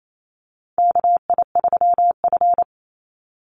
Morse code, heard as a keyed tone, about 28 wpm, 700 Hz. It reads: KI3F
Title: Call Signs Repeated 2x - 28wpm